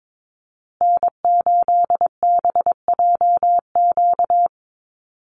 Morse code, heard as a keyed tone, about 22 wpm, 700 Hz. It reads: N8BJQ
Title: Call Signs Repeated 3x - 22wpm